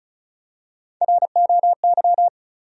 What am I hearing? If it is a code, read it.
Morse code, heard as a keyed tone, about 34 wpm, 700 Hz. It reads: ROY
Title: USA Common Names - Max 4 Letters - 35wpm